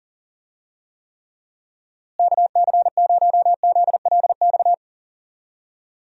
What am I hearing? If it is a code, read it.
Morse code, heard as a keyed tone, about 40 wpm, 700 Hz. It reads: KC0ZLX
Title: USA Call Signs - 6 Characters - Repeated 2x - 40wpm